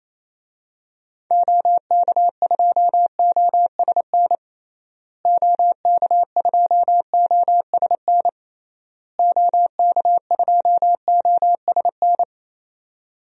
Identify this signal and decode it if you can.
Morse code, heard as a keyed tone, about 28 wpm, 700 Hz. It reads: OK2OSN OK2OSN OK2OSN
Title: DXCC-100 Call Signs - 6 Characters - Repeated 3x - 28wpm